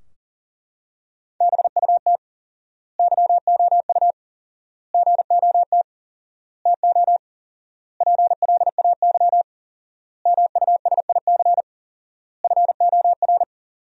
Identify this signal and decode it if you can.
Morse code, heard as a keyed tone, about 40 wpm, 700 Hz. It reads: BUT YOU GOT TO PLAY MUSIC FOR
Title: 4x Word-Spacing - Sentences from Top 300 Words - 40wpm